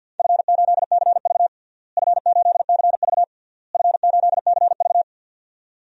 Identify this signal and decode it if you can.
Morse code, heard as a keyed tone, about 53 wpm, 700 Hz. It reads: F8CV F8CV F8CV
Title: DXCC-100 Call Signs - 4 Characters - Repeated 3x - 50wpm